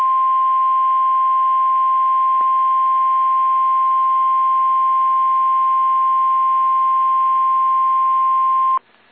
Emergency Alert System (SAME) tones (the 1050 Hz attention tone). It is heard from 0.7 to 8.8 s.
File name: eas / KSDZKDJL / KSDZKDJL